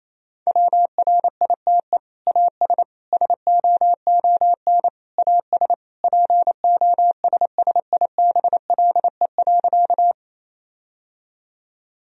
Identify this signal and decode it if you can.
Morse code, heard as a keyed tone, about 28 wpm, 700 Hz. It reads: WRITE AS SOON AS POSSIBLE.
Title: Sentences from Top 1000 Words - Review Part 1 - 28wpm